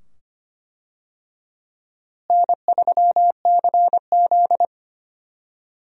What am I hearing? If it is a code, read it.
Morse code, heard as a keyed tone, about 25 wpm, 700 Hz. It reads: N3CZ
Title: USA Call Signs - 4 Characters - Repeated 2x - 25wpm